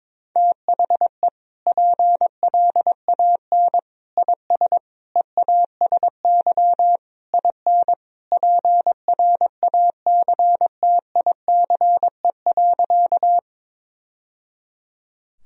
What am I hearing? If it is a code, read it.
Morse code, heard as a keyed tone, about 22 wpm, 700 Hz. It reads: THE PLAN IS EASY IN PRACTICE.
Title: Sentences from Top 1000 Words - Review Part 4 - 22wpm